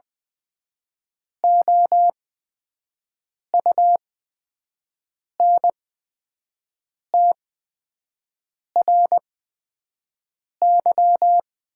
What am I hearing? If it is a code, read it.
Morse code, heard as a keyed tone, about 20 wpm, spaced as for 5 wpm, 700 Hz. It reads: OUNTRY
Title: 8x Character Spacing - Top 300 Words - 20wpm